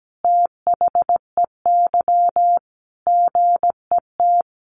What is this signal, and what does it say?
Morse code, heard as a keyed tone, about 17 wpm, 700 Hz. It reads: THEY GET
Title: Sentences from Top 1200 Words 17wpm